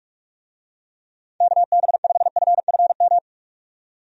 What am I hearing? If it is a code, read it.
Morse code, heard as a keyed tone, about 43 wpm, 700 Hz. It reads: KB5FFM
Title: USA Call Signs - 6 Characters - 45wpm